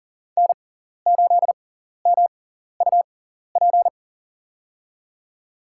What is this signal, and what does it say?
Morse code, heard as a keyed tone, about 40 wpm, spaced as for 14 wpm, 700 Hz. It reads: N8MUP
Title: USA Call Signs - 5 Characters - 6x Character Spacing - 40wpm